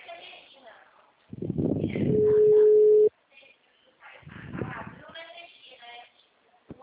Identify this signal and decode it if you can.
Ringback tone (Europe, 425 Hz), with speech.